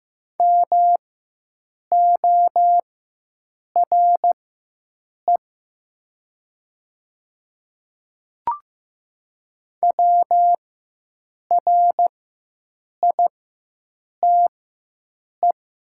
Morse code, heard as a keyed tone, about 15 wpm, spaced as for 7 wpm, 700 Hz. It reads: MORE WRITE
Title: 4x Character Spacing - Top 100 Words - 15wpm